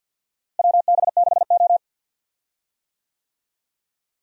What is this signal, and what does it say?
Morse code, heard as a keyed tone, about 49 wpm, 700 Hz. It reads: WB6O
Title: USA Call Signs - 4 Characters - Repeated 2x - 50wpm